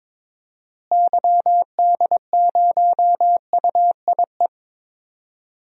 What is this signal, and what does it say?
Morse code, heard as a keyed tone, about 22 wpm, 700 Hz. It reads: YD0UIE
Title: DXCC-100 Call Signs - 6 Characters - Repeated 2x - 22wpm